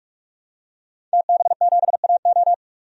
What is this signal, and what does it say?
Morse code, heard as a keyed tone, about 46 wpm, 700 Hz. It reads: TB7AO
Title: DXCC-100 Call Signs - 5 Characters - 45wpm